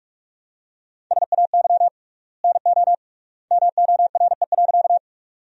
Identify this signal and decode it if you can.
Morse code, heard as a keyed tone, about 45 wpm, 700 Hz. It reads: SAY NO MORE.
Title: 3x Word-Spacing - Sentences from Top 200 Words - 45wpm